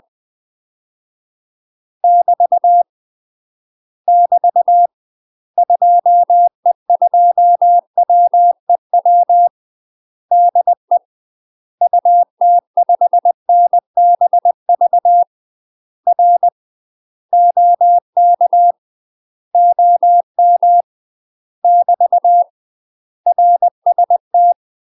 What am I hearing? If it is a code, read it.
Morse code, heard as a keyed tone, about 20 wpm, 700 Hz. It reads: <BT> <BT> 2E2WEW DE UT5NBV R OK OM <BT> RST
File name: UK MASTER QSOs TRAINING / 20WPM / UK MASTER QSOs 0015